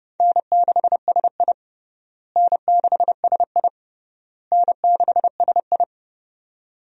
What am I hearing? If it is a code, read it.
Morse code, heard as a keyed tone, about 30 wpm, 700 Hz. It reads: N6SI N6SI N6SI
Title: USA Call Signs - 4 Characters - Repeated 3x - 30wpm